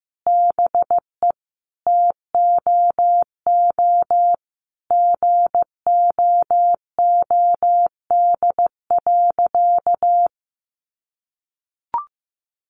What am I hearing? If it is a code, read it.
Morse code, heard as a keyed tone, about 15 wpm, 700 Hz. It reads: BE TOO GOOD.
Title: Sentences from Top 1200 Words 15wpm